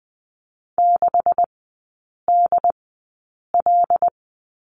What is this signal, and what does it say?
Morse code, heard as a keyed tone, about 20 wpm, spaced as for 8 wpm, 690 Hz. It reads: 6DL
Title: 145 Lesson 24e - Letter y - Three Characters 20wpm